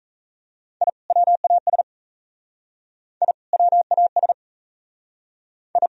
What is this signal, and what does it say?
Morse code, heard as a keyed tone, about 35 wpm, 700 Hz. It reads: I WAS I WAS I